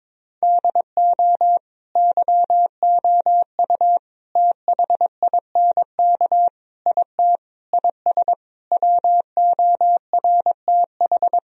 Morse code, heard as a keyed tone, about 22 wpm, 700 Hz. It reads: DO YOU THINK IT IS WORTH
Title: Sentences from Top 1000 Words - Review Part 1 - 22wpm